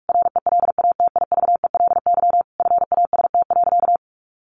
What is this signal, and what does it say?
Morse code, heard as a keyed tone, about 44 wpm, 700 Hz. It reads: RELATIVELY FAST.
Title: Sentences from Top 1100 Words 45wpm